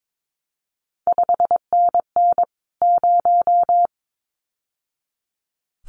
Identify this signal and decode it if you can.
Morse code, heard as a keyed tone, about 22 wpm, 700 Hz. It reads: 5NN 0
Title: CQ WW DX Contest Exchange Encore - 22wpm